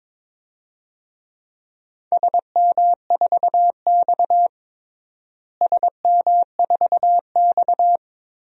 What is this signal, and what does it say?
Morse code, heard as a keyed tone, about 22 wpm, 700 Hz. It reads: SM4X SM4X
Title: DXCC-100 Call Signs - 4 Characters - Repeated 3x - 22wpm